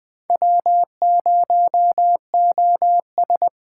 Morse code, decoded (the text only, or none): W0OS